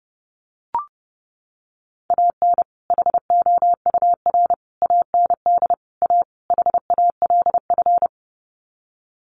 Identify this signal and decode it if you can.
Morse code, heard as a keyed tone, about 30 wpm, 700 Hz. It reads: AN HOUR AND A HALF
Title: Sets of 5 Words 30wpm